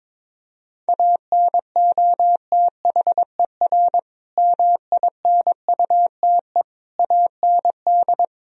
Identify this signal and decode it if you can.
Morse code, heard as a keyed tone, about 22 wpm, 700 Hz. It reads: ANOTHER MINUTE AND
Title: Sentences from Top 1000 Words - Review Part 3 - 22wpm